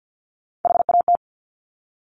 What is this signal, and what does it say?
Morse code, heard as a keyed tone, about 50 wpm, 700 Hz. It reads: HAT
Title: Top Three Letter Combinations in Words 50wpm